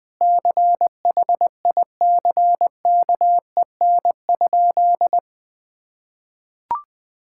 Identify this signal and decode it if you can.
Morse code, heard as a keyed tone, about 20 wpm, 700 Hz. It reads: CHICKEN?
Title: Sentences from Top 1000 Words - Review Part 5 - 20wpm